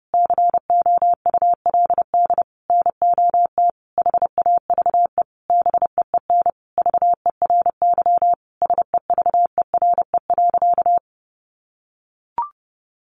Morse code, heard as a keyed tone, about 30 wpm, 700 Hz. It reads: COULD NOT HAVE BEEN VERY SEVERE.
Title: Sentences from Top 1400 Words 30wpm